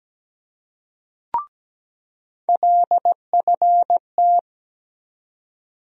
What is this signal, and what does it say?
Morse code, heard as a keyed tone, about 17 wpm, 700 Hz. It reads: LFT
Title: USA Call Sign Postfixes - 3 Letters 17wpm